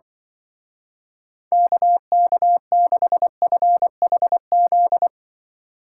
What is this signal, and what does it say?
Morse code, heard as a keyed tone, about 24 wpm, 700 Hz. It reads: KK6FHZ
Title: Speed Racing - USA 2x3 Call Signs 20wpm